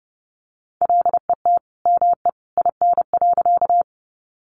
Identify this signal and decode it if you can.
Morse code, heard as a keyed tone, about 30 wpm, 700 Hz. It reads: LET ME IN.
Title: Sentences from Top 300 Words 30wpm